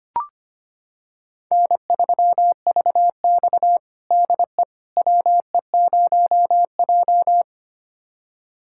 Morse code, heard as a keyed tone, about 25 wpm, 700 Hz. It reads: N3VX DE WE0J
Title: USA Callsign DE Callsign - 4 Characters - 25wpm